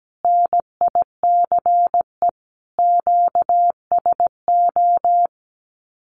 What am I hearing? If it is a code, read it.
Morse code, heard as a keyed tone, about 17 wpm, 700 Hz. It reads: NICE QSO
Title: Common QSO Phrases Encore 17wpm